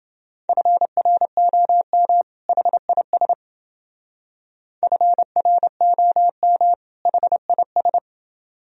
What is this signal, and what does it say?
Morse code, heard as a keyed tone, about 28 wpm, 700 Hz. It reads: FROM HIS FROM HIS